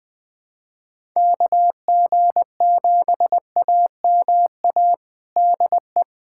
Morse code, heard as a keyed tone, about 20 wpm, 700 Hz. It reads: KG7AMA DE
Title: USA Callsign DE Callsign 20wpm